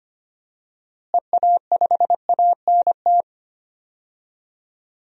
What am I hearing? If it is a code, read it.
Morse code, heard as a keyed tone, about 25 wpm, 700 Hz. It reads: EA5ANT
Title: DXCC-100 Call Signs - 6 Characters - Repeated 2x - 25wpm